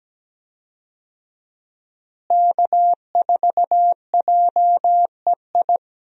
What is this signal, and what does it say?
Morse code, heard as a keyed tone, about 17 wpm, 700 Hz. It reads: K4JEI
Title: USA Call Signs - 5 Characters - Repeated 2x - 17wpm